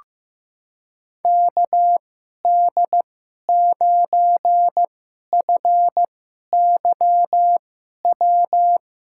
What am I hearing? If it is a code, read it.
Morse code, heard as a keyed tone, about 15 wpm, spaced as for 11 wpm, 700 Hz. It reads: KD9FYW
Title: USA Call Signs - 6 Characters - 2x Character Spacing - 15wpm